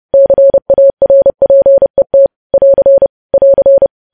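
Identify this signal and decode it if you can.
Morse code, heard as a keyed tone, about 30 wpm, 550 Hz. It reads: CARPET <AR> <AR>
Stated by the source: News headlines sent at 30 wpm.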